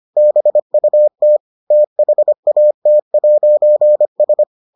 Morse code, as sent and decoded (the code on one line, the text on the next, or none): -... ..- - / - .... .- - .----. ...
BUT THAT'S